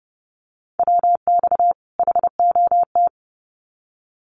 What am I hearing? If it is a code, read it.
Morse code, heard as a keyed tone, about 30 wpm, 700 Hz. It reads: WX HOT
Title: Common QSO Phrases 30wpm